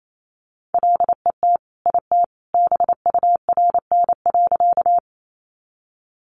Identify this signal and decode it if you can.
Morse code, heard as a keyed tone, about 28 wpm, 700 Hz. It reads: LET IT BURN.
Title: Sentences from Top 1300 Words 28wpm